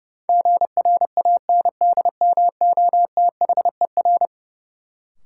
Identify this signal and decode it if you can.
Morse code, heard as a keyed tone, about 32 wpm, 700 Hz. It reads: GRANDMOTHER
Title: Sentences from Top 1200 Words - New Word Drill Alt - 30wpm